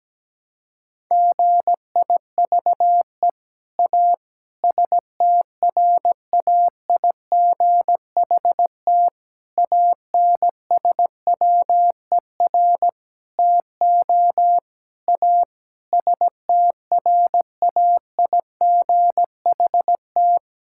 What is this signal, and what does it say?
Morse code, heard as a keyed tone, about 17 wpm, 700 Hz. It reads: GIVE A STRAIGHT ANSWER TO A STRAIGHT
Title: Sentences from Top 1000 Words - Review Part 6 - 17wpm